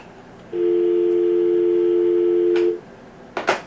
Dial tone (North America), 350 + 440 Hz.